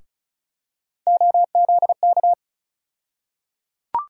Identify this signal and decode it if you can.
Morse code, heard as a keyed tone, about 36 wpm, 700 Hz. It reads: OZK
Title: USA Call Sign Postfixes - 3 Letters 35wpm